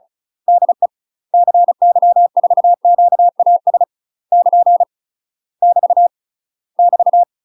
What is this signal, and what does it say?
Morse code, heard as a keyed tone, about 34 wpm, 700 Hz. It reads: DE CY4QAS <KN> <BT> <BT>
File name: UK MASTER QSOs TRAINING / 35WPM / UK MASTER QSOs 0005